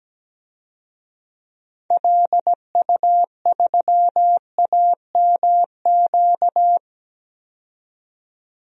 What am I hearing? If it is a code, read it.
Morse code, heard as a keyed tone, about 17 wpm, 700 Hz. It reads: LU3AMQ
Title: DXCC-100 Call Signs - 6 Characters - 17wpm